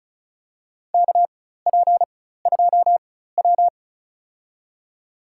Morse code, heard as a keyed tone, about 34 wpm, spaced as for 16 wpm, 700 Hz. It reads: KP2W
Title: USA Call Signs - 4 Characters - 4x Character Spacing - 35wpm